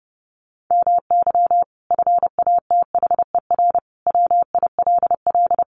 Morse code, heard as a keyed tone, about 30 wpm, 700 Hz. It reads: MY FATHER WILL
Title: Sentences from Top 1100 Words 30wpm